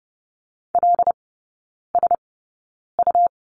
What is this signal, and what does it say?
Morse code, heard as a keyed tone, about 30 wpm, spaced as for 9 wpm, 700 Hz. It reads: LSU
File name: Lesson 19e - Letter w - Three Characters 30wpm